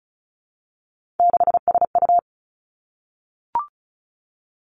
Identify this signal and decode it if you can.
Morse code, heard as a keyed tone, about 36 wpm, 700 Hz. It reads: 6SU
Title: Three Challenging Letter-Number Combinations 35wpm